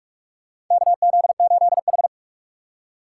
Morse code, heard as a keyed tone, about 46 wpm, 700 Hz. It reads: KZ8H
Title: USA Call Signs - 4 Characters - 2x Character Spacing - 45wpm